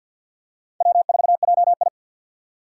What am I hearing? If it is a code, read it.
Morse code, heard as a keyed tone, about 53 wpm, 700 Hz. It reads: W4JI
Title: USA Call Signs - 4 Characters - 3x Character Spacing - 50wpm